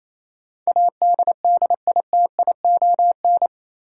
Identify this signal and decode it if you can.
Morse code, heard as a keyed tone, about 27 wpm, 700 Hz. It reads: ADDITION
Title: Sentences from Top 1500 Words - New Word Drill Alt - 28wpm